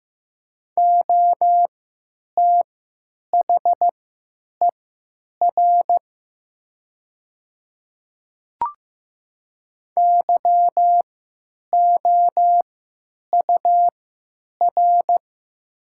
Morse code, heard as a keyed tone, about 15 wpm, spaced as for 9 wpm, 700 Hz. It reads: OTHER YOUR
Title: 3x Character Spacing - Top 100 Words - 15wpm